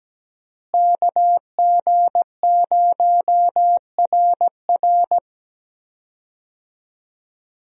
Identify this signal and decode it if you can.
Morse code, heard as a keyed tone, about 17 wpm, 700 Hz. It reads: KG0RR
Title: USA Call Signs - 5 Characters - Repeated 3x - 17wpm